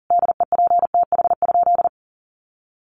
Morse code, heard as a keyed tone, about 40 wpm, 700 Hz. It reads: DEPTH?
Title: Sentences from Top 1200 Words 40wpm